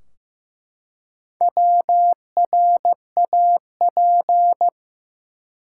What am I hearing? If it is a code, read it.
Morse code, heard as a keyed tone, about 15 wpm, 700 Hz. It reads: WRAP
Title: Four Letter Words - 15wpm